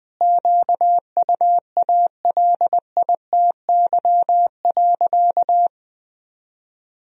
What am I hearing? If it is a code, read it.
Morse code, heard as a keyed tone, about 21 wpm, 700 Hz. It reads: QUALITY.
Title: Sentences from Top 1000 Words - Review Part 4 - 20wpm